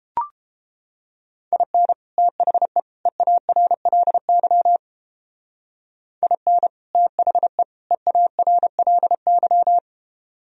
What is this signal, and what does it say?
Morse code, heard as a keyed tone, about 29 wpm, 700 Hz. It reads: IN THE EARLY IN THE EARLY